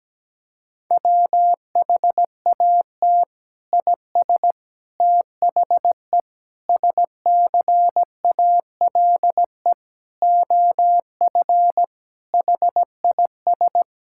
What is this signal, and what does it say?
Morse code, heard as a keyed tone, about 17 wpm, 700 Hz. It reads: WHAT IS THE SCALE OF HIS
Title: Sentences from Top 1000 Words - Review Part 5 - 17wpm